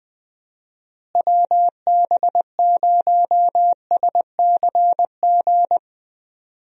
Morse code, heard as a keyed tone, about 20 wpm, 700 Hz. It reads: WB0SCG
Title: USA Call Signs - 6 Characters - Repeated 2x - 20wpm